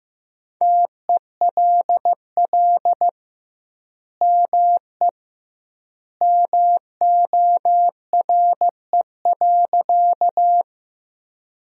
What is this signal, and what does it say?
Morse code, heard as a keyed tone, about 15 wpm, 700 Hz. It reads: TELL ME MORE.
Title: 2x Word-Spacing - Sentences from Top 300 Words - 15wpm